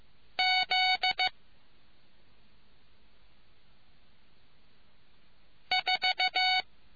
Morse code, heard as a keyed tone, about 15 wpm, 740 Hz. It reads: Z 4